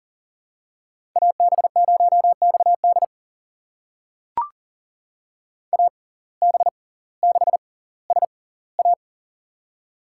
Morse code, heard as a keyed tone, about 40 wpm, spaced as for 20 wpm, 700 Hz. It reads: AB0XD AB6SA
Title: USA Call Signs - 5 Characters - 6x Character Spacing - 40wpm